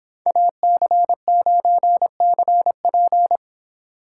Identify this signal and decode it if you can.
Morse code, heard as a keyed tone, about 25 wpm, 700 Hz. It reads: AC9CP